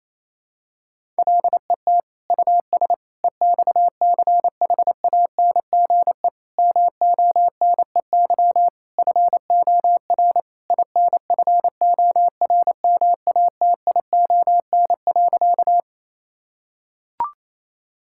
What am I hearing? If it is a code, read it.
Morse code, heard as a keyed tone, about 28 wpm, 700 Hz. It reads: LET US EXCHANGE MONEY FOR INFORMATION.
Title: Sentences from Top 1000 Words - Review Part 6 - 28wpm